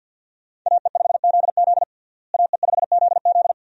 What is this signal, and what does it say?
Morse code, heard as a keyed tone, about 53 wpm, 700 Hz. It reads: AE5ZZ AE5ZZ
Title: USA Call Signs - 5 Characters - Repeated 2x - 50wpm